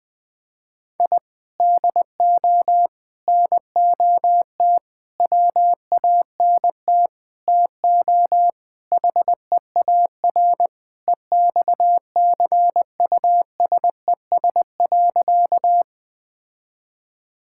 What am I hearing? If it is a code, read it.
Morse code, heard as a keyed tone, about 20 wpm, 700 Hz. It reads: I DO NOT WANT TO HEAR EXCUSES.
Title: Sentences with 5-Word Phrases - Spoken Beginning - 20wpm